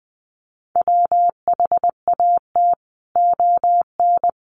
Morse code, heard as a keyed tone, about 20 wpm, 700 Hz. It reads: WHAT ON
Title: Sentences from Top 1100 Words 20wpm